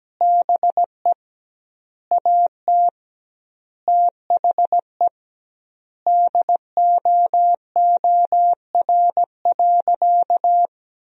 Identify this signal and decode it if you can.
Morse code, heard as a keyed tone, about 17 wpm, 700 Hz. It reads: BE AT THE DOOR.
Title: 2x Word-Spacing - Sentences from Top 400 Words - 17wpm